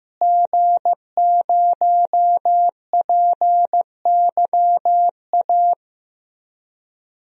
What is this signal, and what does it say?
Morse code, heard as a keyed tone, about 15 wpm, 700 Hz. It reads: G0PYA
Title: DXCC-100 Call Signs - 5 Characters - Repeated 3x - 15wpm